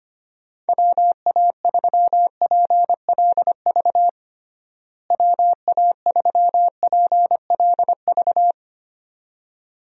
Morse code, heard as a keyed tone, about 25 wpm, 700 Hz. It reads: WA3PLV WA3PLV
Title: USA Call Signs - 6 Characters - Repeated 2x - 25wpm